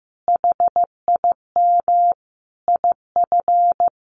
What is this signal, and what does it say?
Morse code, heard as a keyed tone, about 15 wpm, 700 Hz. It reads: HIM IF
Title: Sentences from Top 1400 Words 15wpm